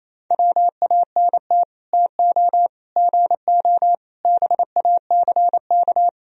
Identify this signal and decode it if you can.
Morse code, heard as a keyed tone, about 37 wpm, 700 Hz. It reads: WANT TO GO BACK